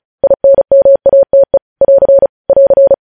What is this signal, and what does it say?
Morse code, heard as a keyed tone, about 36 wpm, 550 Hz. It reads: INMATE <AR> <AR>
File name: podcast / News headlines-1x-35x35